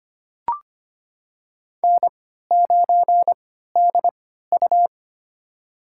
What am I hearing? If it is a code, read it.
Morse code, heard as a keyed tone, about 25 wpm, spaced as for 14 wpm, 700 Hz. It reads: N9DU
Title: USA Call Signs - 4 Characters - 3x Character Spacing - 25wpm